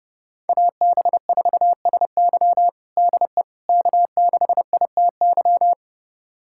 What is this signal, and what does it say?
Morse code, heard as a keyed tone, about 30 wpm, 700 Hz. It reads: AB4SY DE K6ITY
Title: USA Callsign DE Callsign - 5 Characters - 30wpm